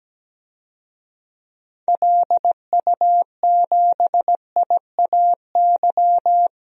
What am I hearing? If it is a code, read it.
Morse code, heard as a keyed tone, about 17 wpm, 700 Hz. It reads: LU7IAY
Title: DXCC-100 Call Signs - 6 Characters - 17wpm